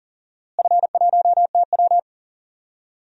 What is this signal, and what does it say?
Morse code, heard as a keyed tone, about 40 wpm, 700 Hz. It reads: F1TW
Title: DXCC-100 Call Signs - 4 Characters - Repeated 2x - 40wpm